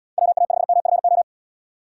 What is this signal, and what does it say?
Morse code, heard as a keyed tone, about 73 wpm, 700 Hz. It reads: WA5MFO